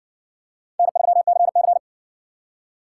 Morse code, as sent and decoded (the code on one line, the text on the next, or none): -. ...-- -.-. -.-.
N3CC